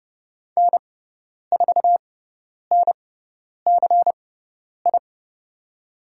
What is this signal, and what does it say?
Morse code, heard as a keyed tone, about 30 wpm, spaced as for 10 wpm, 700 Hz. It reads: N4NCI
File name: Lesson 18g - Letter m - Call Signs 30wpm